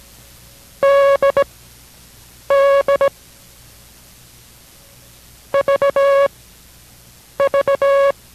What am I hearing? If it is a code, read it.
Morse code, heard as a keyed tone, about 17 wpm, spaced as for 6 wpm, 560 Hz. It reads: DD VV